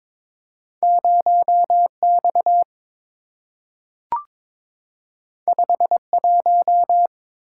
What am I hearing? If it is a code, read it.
Morse code, heard as a keyed tone, about 22 wpm, 700 Hz. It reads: 0X 51
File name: practice / 22wpm / Two Letter-Number Combinations - 22wpm